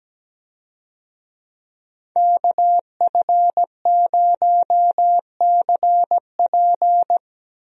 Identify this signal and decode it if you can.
Morse code, heard as a keyed tone, about 17 wpm, 700 Hz. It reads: KF0CP
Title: USA Call Signs - 5 Characters - Repeated 3x - 17wpm